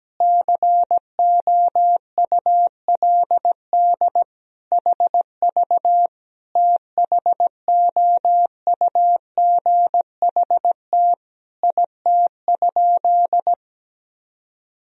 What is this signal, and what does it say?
Morse code, heard as a keyed tone, about 17 wpm, 700 Hz. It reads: COULD HV THOUGHT IT?
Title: Sentences from Top 300 Words and Abbreviations 17wpm